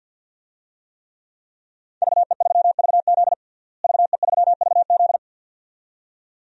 Morse code, heard as a keyed tone, about 52 wpm, 700 Hz. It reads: VE3VZ VE3VZ
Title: DXCC-100 Call Signs - 5 Characters - Repeated 2x - 50wpm